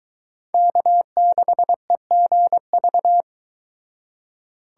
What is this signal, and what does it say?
Morse code, heard as a keyed tone, about 23 wpm, 700 Hz. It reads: K6EGV